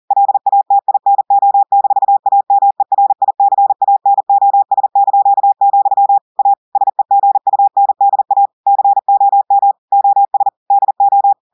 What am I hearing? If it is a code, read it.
Morse code, heard as a keyed tone, about 40 wpm, 800 Hz. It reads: LATINO-AMERICANOS), A SEGUNDA COM OS DO
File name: audios / portugais / portugais texte en clair 1 vitesse 40 mots minutes